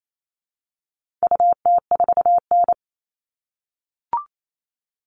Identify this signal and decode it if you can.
Morse code, heard as a keyed tone, about 28 wpm, 700 Hz. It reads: UT4N
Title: All Cut Numbers - 4-Digit 28wpm